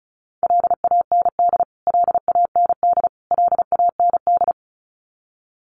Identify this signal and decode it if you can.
Morse code, heard as a keyed tone, about 34 wpm, 700 Hz. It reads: LAND LAND LAND
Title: Top 200 Words Repeated 35wpm